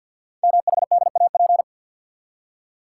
Morse code, heard as a keyed tone, about 51 wpm, 700 Hz. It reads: MHDAP
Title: Code Groups - 5 Letters - 50wpm